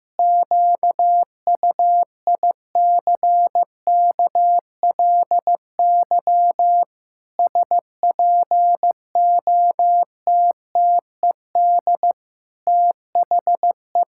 Morse code, heard as a keyed tone, about 15 wpm, 700 Hz. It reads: QUICKLY SPOTTED THE
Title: Pangram Challenge v2 - 15wpm